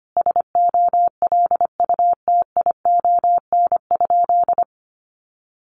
Morse code, heard as a keyed tone, about 25 wpm, 700 Hz. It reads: SOLUTION?
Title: Sentences from Top 1300 Words 25wpm